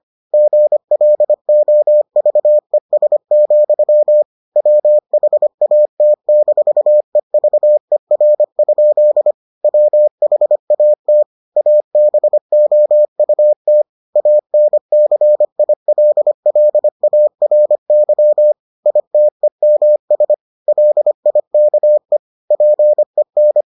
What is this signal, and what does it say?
Morse code, heard as a keyed tone, about 25 wpm, 600 Hz. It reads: GLOVES, WHAT-EVER? WHAT ABOUT ANCILLARY ITEMS LIKE PEN